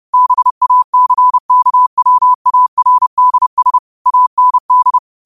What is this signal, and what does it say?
Morse code, heard as a keyed tone, about 30 wpm, 1000 Hz. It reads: BACKWARDS AND